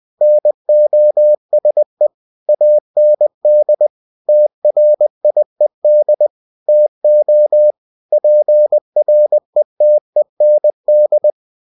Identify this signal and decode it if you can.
Morse code, heard as a keyed tone, about 20 wpm, 600 Hz. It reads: NOSE AND TRIED TO PRETEND
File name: cw-books / winnie-the-pooh / winnie-20wpm-page-0099